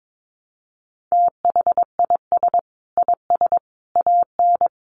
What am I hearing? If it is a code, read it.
Morse code, heard as a keyed tone, about 22 wpm, 700 Hz. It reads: THIS IS AN
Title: Sentences from Top 900 Words 22wpm